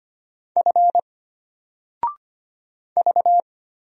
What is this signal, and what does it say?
Morse code, heard as a keyed tone, about 25 wpm, 700 Hz. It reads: F V
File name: Single-Letter-Number-25wpm